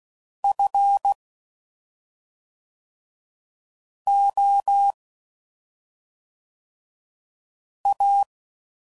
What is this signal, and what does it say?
Morse code, heard as a keyed tone, about 16 wpm, 800 Hz. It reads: F O A
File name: Suffixes / Special ING's / SlowToFaster ING's